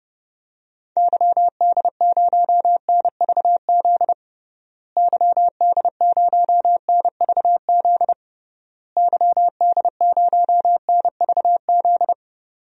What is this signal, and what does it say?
Morse code, heard as a keyed tone, about 32 wpm, 700 Hz. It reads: YD0NVZ YD0NVZ YD0NVZ
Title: DXCC-100 Call Signs - 6 Characters - Repeated 3x - 30wpm